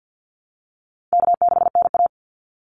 Morse code, heard as a keyed tone, about 50 wpm, 700 Hz. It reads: K6NA
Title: Easy Call Signs Repeated 3x - 50wpm